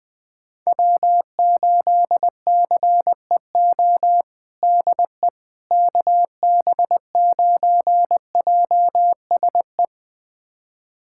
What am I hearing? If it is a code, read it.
Morse code, heard as a keyed tone, about 20 wpm, 700 Hz. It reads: W8CEO DE KB9JSE
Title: USA Callsign DE Callsign 20wpm